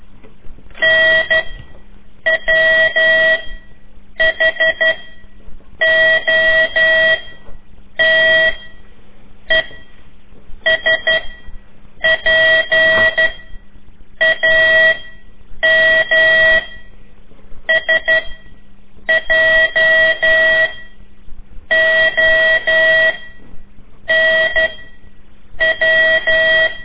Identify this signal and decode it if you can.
Morse code, heard as a keyed tone, about 12 wpm, spaced as for 7 wpm, 1950 Hz. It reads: NWHOTESPAMSJONW
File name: ik1weg2 / CORSI CW / lez4-3